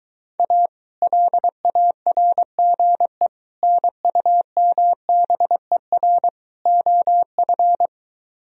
Morse code, heard as a keyed tone, about 23 wpm, 700 Hz. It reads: A LARGE NUMBER OF